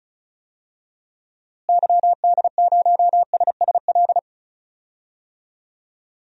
Morse code, heard as a keyed tone, about 36 wpm, 700 Hz. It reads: YD0SSL
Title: DXCC-100 Call Signs - 6 Characters - Repeated 2x - 35wpm